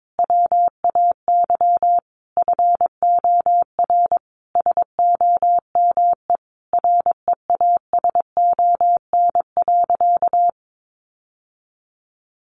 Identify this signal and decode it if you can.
Morse code, heard as a keyed tone, about 22 wpm, 700 Hz. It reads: WAY FOR SOME REASON.
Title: Sentences from Top 700 Words 22wpm